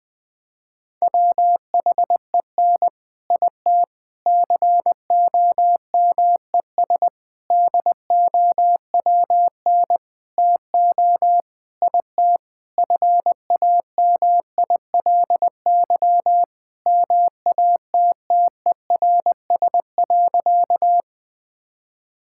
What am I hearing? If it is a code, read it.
Morse code, heard as a keyed tone, about 20 wpm, 700 Hz. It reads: WHEN IT COMES DOWN TO IT FAMILY MATTERS.
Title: Sentences with 6-Word Phrases - 20wpm